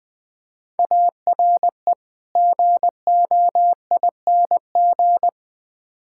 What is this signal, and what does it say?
Morse code, heard as a keyed tone, about 21 wpm, 700 Hz. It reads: ARE GOING
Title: Build A Phrase - 4-Word Phrases - 20wpm